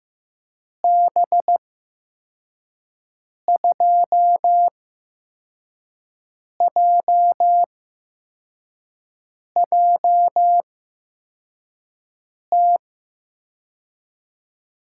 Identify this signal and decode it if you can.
Morse code, heard as a keyed tone, about 15 wpm, spaced as for 4 wpm, 700 Hz. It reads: B2JJT
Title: USA Call Signs - 6 Characters - 8x Character Spacing - 15wpm